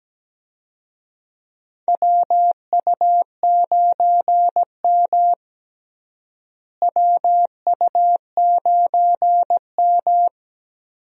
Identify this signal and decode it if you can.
Morse code, heard as a keyed tone, about 17 wpm, 700 Hz. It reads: WU9M WU9M
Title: USA Call Signs - 4 Characters - Repeated 2x - 17wpm